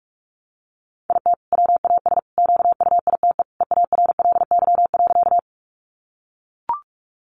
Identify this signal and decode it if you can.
Morse code, heard as a keyed tone, about 44 wpm, 700 Hz. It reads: IT WAS QUITE EARLY.
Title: Sentences from Top 700 Words 45wpm